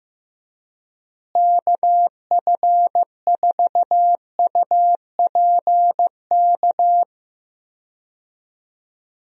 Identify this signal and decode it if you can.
Morse code, heard as a keyed tone, about 15 wpm, 700 Hz. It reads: KF4UPK
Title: USA Call Signs - 6 Characters - Repeated 3x - 15wpm